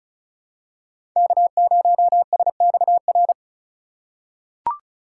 Morse code, heard as a keyed tone, about 35 wpm, 700 Hz. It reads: K0SXR
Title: USA Call Signs - 5 Characters - 2x Character Spacing - 35wpm